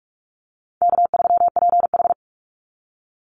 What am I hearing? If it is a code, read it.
Morse code, heard as a keyed tone, about 44 wpm, 700 Hz. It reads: K3PH
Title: Call Signs Repeated 2x - 45wpm